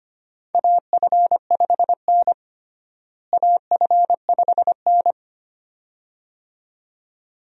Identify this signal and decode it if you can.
Morse code, heard as a keyed tone, about 25 wpm, 700 Hz. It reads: AF5N AF5N
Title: USA Call Signs - 4 Characters - Repeated 2x - 25wpm